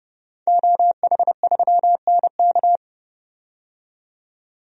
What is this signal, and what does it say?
Morse code, heard as a keyed tone, about 30 wpm, 700 Hz. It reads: OH3NK
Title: DXCC-100 Call Signs - 5 Characters - 30wpm